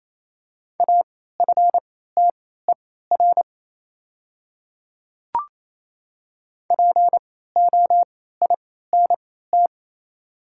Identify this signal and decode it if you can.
Morse code, heard as a keyed tone, about 28 wpm, spaced as for 16 wpm, 700 Hz. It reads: AFTER POINT
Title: 3x Character Spacing - Top 300 Words - 28wpm